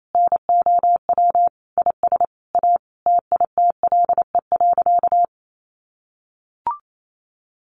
Morse code, heard as a keyed tone, about 28 wpm, 700 Hz. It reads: NOW IS A TITLE.
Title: Sentences from Top 1200 Words 28wpm